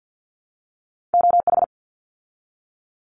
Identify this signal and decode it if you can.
Morse code, heard as a keyed tone, about 48 wpm, 700 Hz. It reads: OH
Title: ARRL Sections 50wpm